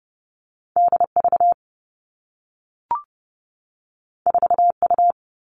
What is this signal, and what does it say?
Morse code, heard as a keyed tone, about 30 wpm, 700 Hz. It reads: D V 4 U
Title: Two Challenging Letter-Number Combinations 30wpm